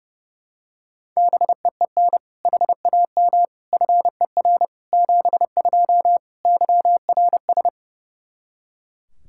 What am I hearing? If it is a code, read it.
Morse code, heard as a keyed tone, about 30 wpm, 700 Hz. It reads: BEEN HAM FER 72 YRS
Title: Practice QSO - Exchange 3 - More Info - 30wpm